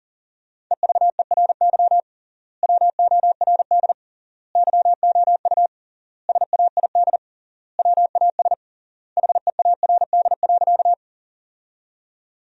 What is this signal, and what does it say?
Morse code, heard as a keyed tone, about 40 wpm, 700 Hz. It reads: EVERY WORD YOU SAID WAS HEARD.
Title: 3x Word-Spacing - Sentences from Top 400 Words - 40wpm